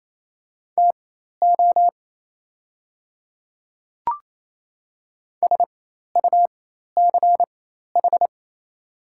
Morse code, heard as a keyed tone, about 28 wpm, spaced as for 13 wpm, 690 Hz. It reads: TO SUCH
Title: 4x Character Spacing - Top 200 Words - 28wpm